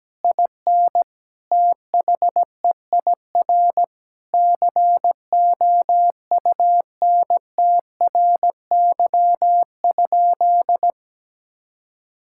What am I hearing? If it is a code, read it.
Morse code, heard as a keyed tone, about 17 wpm, 700 Hz. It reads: IN THEIR COUNTRY?
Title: Sentences from Top 1000 Words - Review Part 5 - 17wpm